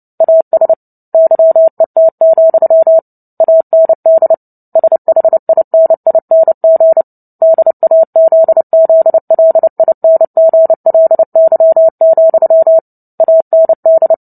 Morse code, heard as a keyed tone, about 30 wpm, 650 Hz. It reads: AS YET, AND SHINING DAZZLINGLY, AND
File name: Book Two / 03-Chapter 3 - 29 WPM